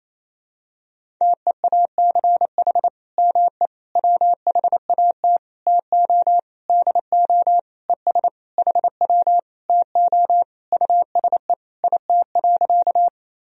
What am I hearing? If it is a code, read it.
Morse code, heard as a keyed tone, about 28 wpm, 700 Hz. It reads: TEACH ME WHAT TO DO ES HW TO USE IT.
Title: Sentences from Top 500 Words and Abbreviations 28wpm